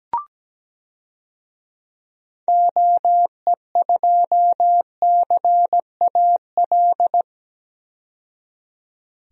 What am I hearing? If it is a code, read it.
Morse code, heard as a keyed tone, about 17 wpm, 700 Hz. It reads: OE2CAL
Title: DXCC-100 Call Signs - 6 Characters - 17wpm